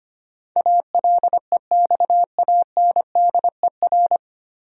Morse code, heard as a keyed tone, about 25 wpm, 700 Hz. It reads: ALEXANDER
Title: USA Common Names - 25wpm